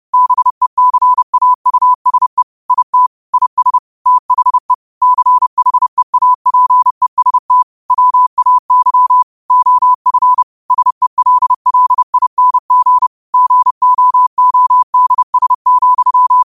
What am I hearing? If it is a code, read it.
Morse code, heard as a keyed tone, about 30 wpm, 1000 Hz. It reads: BECAUSE IT IS THE CHEAPEST WAY OF SELLING GOODS,
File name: qotdfiles / 30wpm